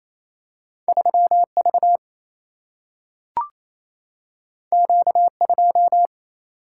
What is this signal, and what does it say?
Morse code, heard as a keyed tone, about 28 wpm, 700 Hz. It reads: 3 V Q 2
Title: Single-Letter Single-Number 28wpm